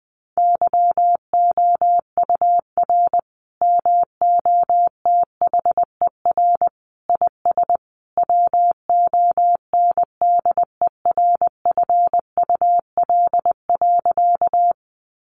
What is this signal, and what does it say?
Morse code, heard as a keyed tone, about 20 wpm, 700 Hz. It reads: YOUR MOTHER IS WONDERFUL.
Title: Sentences from Top 1100 Words 20wpm